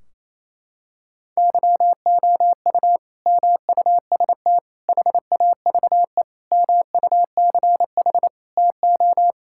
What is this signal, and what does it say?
Morse code, heard as a keyed tone, about 28 wpm, 700 Hz. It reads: YOU MUST HAVE MUCH TO
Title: Sentences from Top 1600 Words 28wpm